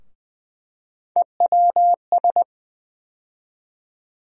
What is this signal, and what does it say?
Morse code, heard as a keyed tone, about 20 wpm, 700 Hz. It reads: EWS
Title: USA Call Sign Postfixes - 3 Letters 20wpm